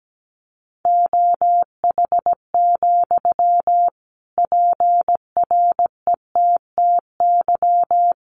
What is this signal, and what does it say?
Morse code, heard as a keyed tone, about 17 wpm, 700 Hz. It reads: OH, PRETTY
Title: Sentences from Top 1300 Words 17wpm